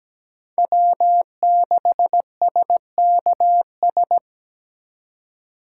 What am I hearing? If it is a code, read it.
Morse code, heard as a keyed tone, about 20 wpm, 700 Hz. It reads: W6SKS
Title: Speed Racing - USA 1x3 Call Signs 17wpm